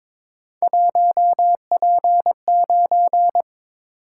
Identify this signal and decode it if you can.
Morse code, heard as a keyed tone, about 22 wpm, 700 Hz. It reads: 1P9
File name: practice / 22wpm / Three Letter-Number Combinations - 22wpm